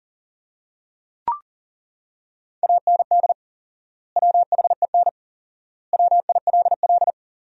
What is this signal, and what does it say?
Morse code, heard as a keyed tone, about 40 wpm, 700 Hz. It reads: AND WHEN WILL
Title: 4x Word-Spacing - Sentences from Top 100 Words - 40wpm